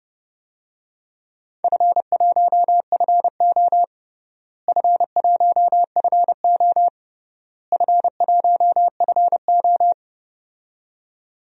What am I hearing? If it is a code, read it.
Morse code, heard as a keyed tone, about 30 wpm, 700 Hz. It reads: F1FO F1FO F1FO
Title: DXCC-100 Call Signs - 4 Characters - Repeated 3x - 30wpm